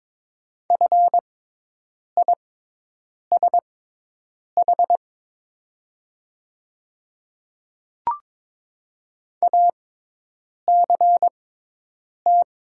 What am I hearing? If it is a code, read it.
Morse code, heard as a keyed tone, about 22 wpm, spaced as for 8 wpm, 700 Hz. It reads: FISH ACT
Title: 6x Character Spacing - Top 300 Words - 22wpm